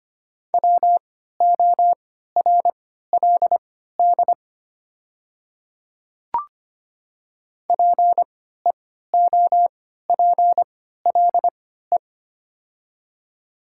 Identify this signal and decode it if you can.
Morse code, heard as a keyed tone, about 25 wpm, spaced as for 14 wpm, 700 Hz. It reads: WORLD PEOPLE
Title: 3x Character Spacing - Top 200 Words - 25wpm